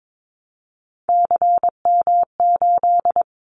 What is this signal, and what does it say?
Morse code, heard as a keyed tone, about 22 wpm, 700 Hz. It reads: CM8
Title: Callsign Prefixes for Top 100 DXCC - 22wpm